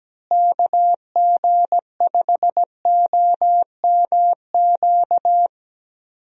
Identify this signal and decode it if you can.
Morse code, heard as a keyed tone, about 17 wpm, 700 Hz. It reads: KG5OMQ
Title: USA Call Signs - 6 Characters - Repeated 2x - 17wpm